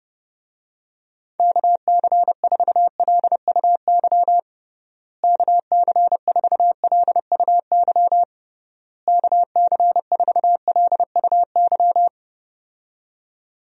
Morse code, heard as a keyed tone, about 30 wpm, 700 Hz. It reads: KC4LUY KC4LUY KC4LUY
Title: USA Call Signs - 6 Characters - Repeated 3x - 30wpm